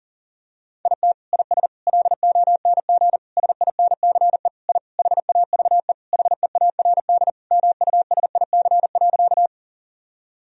Morse code, heard as a keyed tone, about 39 wpm, 700 Hz. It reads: IT IS LONG SINCE I HAVE HEARD MUSIC.
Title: Sentences from Top 1000 Words - Review Part 1 - 40wpm